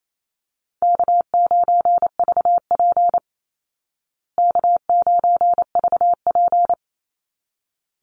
Morse code, heard as a keyed tone, about 28 wpm, 700 Hz. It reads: K9VP K9VP
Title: Easy Call Signs Repeated 2x - 28wpm